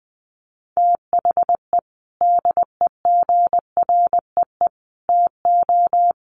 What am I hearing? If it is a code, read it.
Morse code, heard as a keyed tone, about 20 wpm, 700 Hz. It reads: THE DEGREE TO
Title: Sets of 5 Words Encore2 20wpm